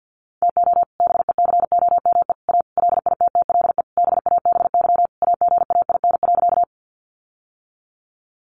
Morse code, heard as a keyed tone, about 50 wpm, 700 Hz. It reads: TO BECOME A LITTLE BABY AGAIN.
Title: Sentences from Top 1300 Words 50wpm